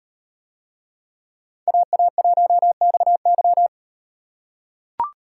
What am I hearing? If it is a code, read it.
Morse code, heard as a keyed tone, about 37 wpm, 700 Hz. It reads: AA1XY